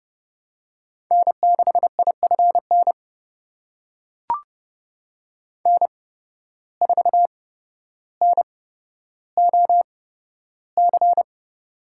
Morse code, heard as a keyed tone, about 30 wpm, spaced as for 13 wpm, 700 Hz. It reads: N6IFN N4NOC
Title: USA Call Signs - 5 Characters - 8x Character Spacing - 30wpm